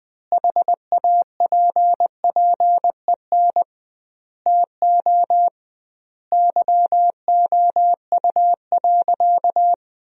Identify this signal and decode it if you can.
Morse code, heard as a keyed tone, about 20 wpm, 700 Hz. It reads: HAPPEN TO YOU.
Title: 2x Word-Spacing - Sentences from Top 500 Words - 20wpm